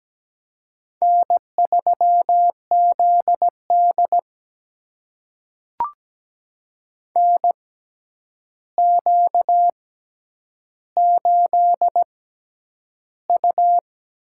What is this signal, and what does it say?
Morse code, heard as a keyed tone, about 17 wpm, spaced as for 9 wpm, 700 Hz. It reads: N3ZD NQ8U
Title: USA Call Signs - 4 Characters - 6x Character Spacing - 17wpm